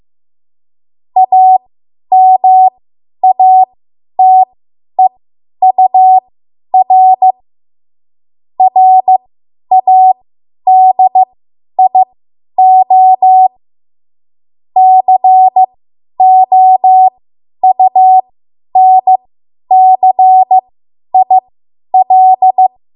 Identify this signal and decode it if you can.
Morse code, heard as a keyed tone, about 15 wpm, spaced as for 10 wpm, 750 Hz. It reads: AMATEUR RADIO COUNCIL
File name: Morse / 10WPM